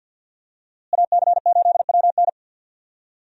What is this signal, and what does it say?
Morse code, heard as a keyed tone, about 52 wpm, 700 Hz. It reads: AC8WN
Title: USA Call Signs - 5 Characters - Repeated 2x - 50wpm